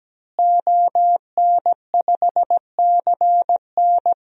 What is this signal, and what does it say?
Morse code, heard as a keyed tone, about 17 wpm, 700 Hz. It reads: ON5CN
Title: DXCC-100 Call Signs - 5 Characters - Repeated 3x - 17wpm